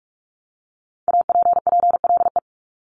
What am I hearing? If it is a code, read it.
Morse code, heard as a keyed tone, about 46 wpm, 700 Hz. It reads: APPLE
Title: Multi-letter Phonemes Repeated 3x 45wpm